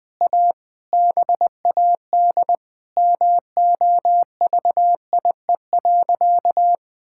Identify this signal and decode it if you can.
Morse code, heard as a keyed tone, about 20 wpm, 700 Hz. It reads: A BAD MOVIE.
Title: Sentences from Top 1000 Words - Review Part 5 - 20wpm